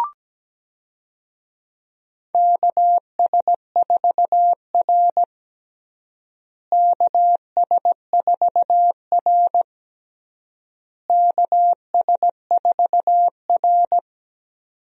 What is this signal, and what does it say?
Morse code, heard as a keyed tone, about 17 wpm, 700 Hz. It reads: KS4R KS4R KS4R
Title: USA Call Signs - 4 Characters - Repeated 3x - 17wpm